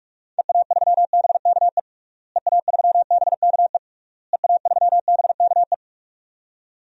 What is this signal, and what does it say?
Morse code, heard as a keyed tone, about 42 wpm, 700 Hz. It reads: EA3BKE EA3BKE EA3BKE
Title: DXCC-100 Call Signs - 6 Characters - Repeated 3x - 45wpm